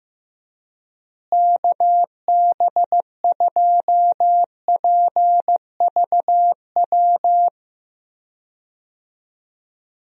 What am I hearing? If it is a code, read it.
Morse code, heard as a keyed tone, about 15 wpm, 700 Hz. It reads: KB2PVW
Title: USA Call Signs - 6 Characters - Repeated 2x - 15wpm